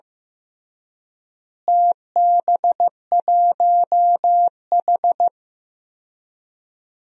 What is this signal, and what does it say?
Morse code, heard as a keyed tone, about 15 wpm, 700 Hz. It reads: TB1H
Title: DXCC-100 Call Signs - 4 Characters - Repeated 3x - 15wpm